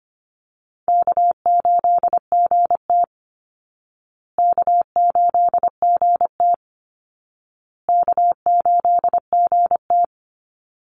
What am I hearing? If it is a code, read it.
Morse code, heard as a keyed tone, about 25 wpm, 700 Hz. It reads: K8GT K8GT K8GT
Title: Easy Call Signs Repeated 3x - 25wpm